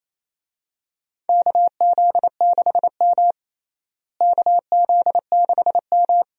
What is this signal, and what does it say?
Morse code, heard as a keyed tone, about 28 wpm, 700 Hz. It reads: KZ6M KZ6M
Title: USA Call Signs - 4 Characters - Repeated 3x - 28wpm